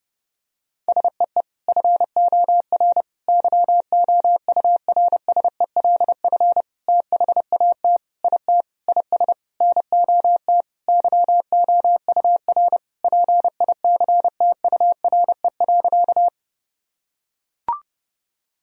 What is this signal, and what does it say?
Morse code, heard as a keyed tone, about 32 wpm, 700 Hz. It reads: SEE FOR YOURSELF THAT IT IS NOT YOUR PICTURE.
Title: Sentences from Top 1000 Words - Review Part 1 - 30wpm